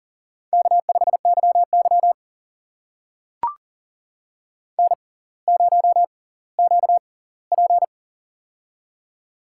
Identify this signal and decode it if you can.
Morse code, heard as a keyed tone, about 39 wpm, spaced as for 20 wpm, 700 Hz. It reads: K5YY N0QP
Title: USA Call Signs - 4 Characters - 6x Character Spacing - 40wpm